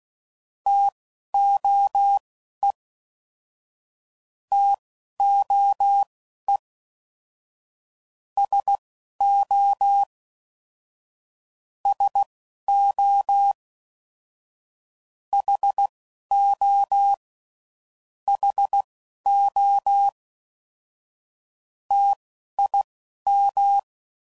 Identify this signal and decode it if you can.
Morse code, heard as a keyed tone, about 16 wpm, spaced as for 11 wpm, 800 Hz. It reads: TOE TOE SO SO HO HO TIM
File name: DIT & DAH Words